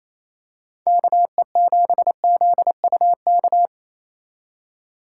Morse code, heard as a keyed tone, about 27 wpm, 700 Hz. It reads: KE7ZUK
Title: USA Call Signs - 6 Characters - 3x Character Spacing - 28wpm